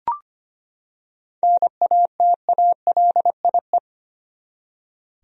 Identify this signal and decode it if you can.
Morse code, heard as a keyed tone, about 25 wpm, 700 Hz. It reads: NATALIE
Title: USA Common Names - 25wpm